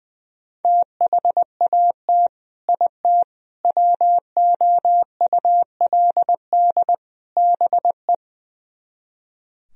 Morse code, heard as a keyed tone, about 20 wpm, 700 Hz. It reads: THAT IT WOULD BE
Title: Speed-Racing - Sets of 4 Words 20wpm